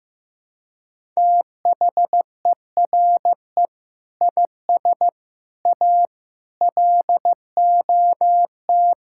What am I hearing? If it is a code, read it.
Morse code, heard as a keyed tone, about 14 wpm, 700 Hz. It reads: THERE IS A LOT
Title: Sentences from Top 1000 Words - Review Part 6 - 15wpm